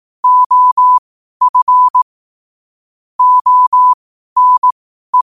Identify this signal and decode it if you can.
Morse code, heard as a keyed tone, about 18 wpm, spaced as for 12 wpm, 1000 Hz. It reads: OF ONE
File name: qotdfiles / 12wpm